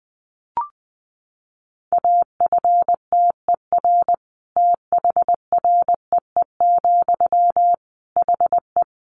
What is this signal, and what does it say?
Morse code, heard as a keyed tone, about 20 wpm, 700 Hz. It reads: AFTER THREE, HE
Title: Sentences from Top 200 Words 20wpm